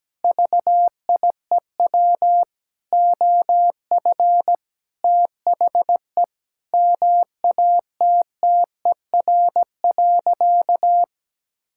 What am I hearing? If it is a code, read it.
Morse code, heard as a keyed tone, about 17 wpm, 700 Hz. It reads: VIEW OF THE MATTER.
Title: Sentences from Top 1600 Words 17wpm